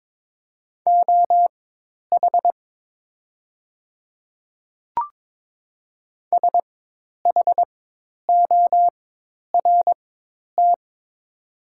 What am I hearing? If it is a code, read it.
Morse code, heard as a keyed tone, about 22 wpm, spaced as for 10 wpm, 700 Hz. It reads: OH SHORT
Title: 4x Character Spacing - Top 500 Words - 22wpm